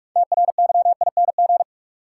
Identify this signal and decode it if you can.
Morse code, heard as a keyed tone, about 46 wpm, 700 Hz. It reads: TRYING
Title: Sets of 2 Words - New Word Drill - 45wpm